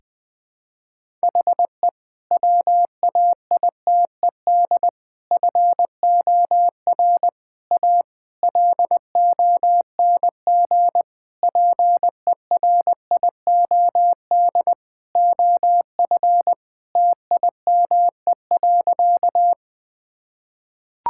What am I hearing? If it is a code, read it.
Morse code, heard as a keyed tone, about 20 wpm, 700 Hz. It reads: HE WAITED FOR A LONG PERIOD OF TIME.
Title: Sentences with 5-Word Phrases - Spoken Beginning - 20wpm